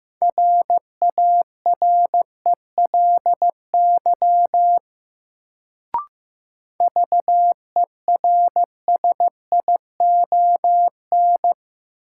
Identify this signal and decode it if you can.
Morse code, heard as a keyed tone, about 16 wpm, 700 Hz. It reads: RARELY VERSION
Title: Sentences from Top 1400 Words - New Word Drill - 15wpm